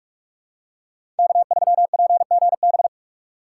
Morse code, heard as a keyed tone, about 46 wpm, 700 Hz. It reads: K3PGB
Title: USA Call Signs - 5 Characters - 6x Character Spacing - 45wpm